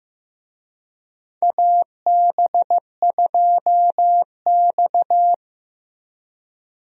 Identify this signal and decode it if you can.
Morse code, heard as a keyed tone, about 15 wpm, 700 Hz. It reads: AB2X
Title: USA Call Signs - 4 Characters - Repeated 2x - 15wpm